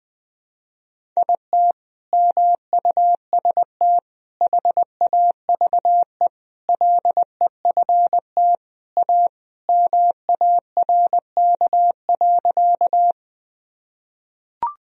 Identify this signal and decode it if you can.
Morse code, heard as a keyed tone, about 20 wpm, 700 Hz. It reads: IT MUST HAVE LEFT A MARK.
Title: Sentences from Top 1000 Words - Review Part 1 - 20wpm